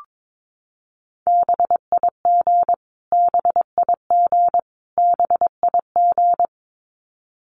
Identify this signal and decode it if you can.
Morse code, heard as a keyed tone, about 22 wpm, 700 Hz. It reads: BIG BIG BIG
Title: Top 200 Words Repeated 22wpm